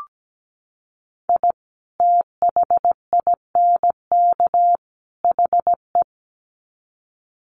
Morse code, heard as a keyed tone, about 17 wpm, 700 Hz. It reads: I THINK HE
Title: Sets of 3 Words 17wpm